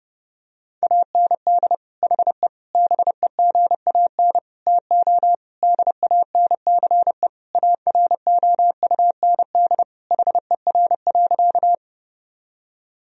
Morse code, heard as a keyed tone, about 30 wpm, 700 Hz. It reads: AND HE BEGAN TO DANCE AROUND HER.
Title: Sentences from Top 1000 Words - Review Part 4 - 30wpm